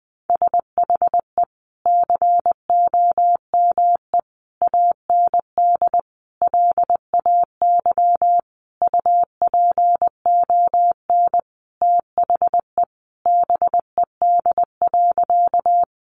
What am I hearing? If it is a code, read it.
Morse code, heard as a keyed tone, about 20 wpm, 700 Hz. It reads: SHE COME AND LAY UPON THE BED.
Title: Sentences from Top 1100 Words 20wpm